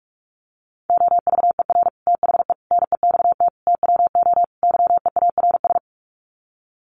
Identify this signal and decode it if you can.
Morse code, heard as a keyed tone, about 44 wpm, 700 Hz. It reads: OVER THE NEXT TWO YEARS
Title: Sets of 5 Words Encore2 45wpm